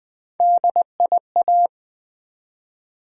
Morse code, heard as a keyed tone, about 20 wpm, 700 Hz. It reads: DIA
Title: Build A Word - 1 Letter At A Time - 5-Letter Words - 20wpm